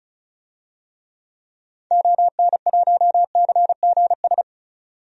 Morse code, heard as a keyed tone, about 36 wpm, 700 Hz. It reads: ON1CGS
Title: DXCC-100 Call Signs - 6 Characters - 35wpm